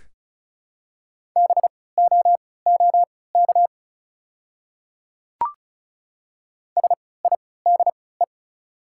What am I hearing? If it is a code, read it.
Morse code, heard as a keyed tone, about 34 wpm, spaced as for 20 wpm, 690 Hz. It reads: BOOK SIDE
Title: 3x Character Spacing - Top 300 Words - 35wpm